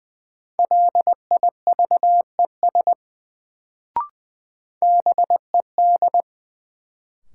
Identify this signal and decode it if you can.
Morse code, heard as a keyed tone, about 20 wpm, 700 Hz. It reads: LIVES BED
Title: Sets of 5 Words Encore - New Word Drill - 20wpm